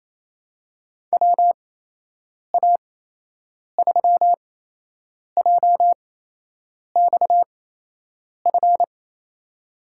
Morse code, heard as a keyed tone, about 28 wpm, spaced as for 8 wpm, 700 Hz. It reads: WA3JXF
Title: USA Call Signs - 6 Characters - 8x Character Spacing - 28wpm